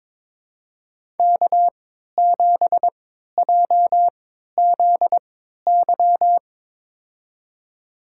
Morse code, heard as a keyed tone, about 22 wpm, spaced as for 13 wpm, 700 Hz. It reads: K7JZY
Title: USA Call Signs - 5 Characters - 3x Character Spacing - 22wpm